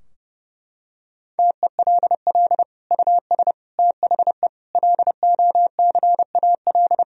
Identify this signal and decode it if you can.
Morse code, heard as a keyed tone, about 30 wpm, 700 Hz. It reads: TELL US THE LOCAL
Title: Sentences from Top 1000 Words - Review Part 3 - 30wpm